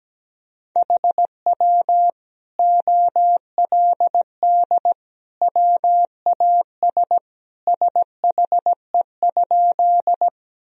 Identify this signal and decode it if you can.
Morse code, heard as a keyed tone, about 17 wpm, 700 Hz. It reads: HW OLD WAS SHE?
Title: Sentences from Top 200 Words and Abbreviations 17wpm